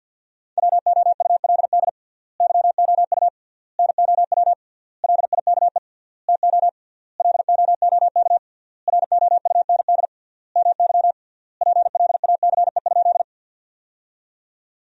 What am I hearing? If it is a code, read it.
Morse code, heard as a keyed tone, about 49 wpm, 700 Hz. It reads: WOULD YOU NOW LIKE TO LOOK ROUND MY PLACE?
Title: 3x Word-Spacing - Sentences from Top 200 Words - 50wpm